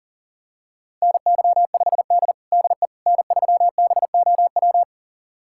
Morse code, heard as a keyed tone, about 40 wpm, 700 Hz. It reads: NY5D DE N3BOW
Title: USA Callsign DE Callsign 40wpm